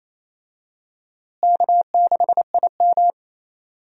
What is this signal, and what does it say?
Morse code, heard as a keyed tone, about 28 wpm, 700 Hz. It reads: K6IM
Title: USA Call Signs - 4 Characters - Repeated 3x - 28wpm